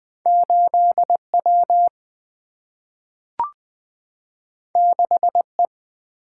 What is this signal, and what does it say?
Morse code, heard as a keyed tone, about 20 wpm, 700 Hz. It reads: 8 W 6 E
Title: Single-Letter Single-Number 20wpm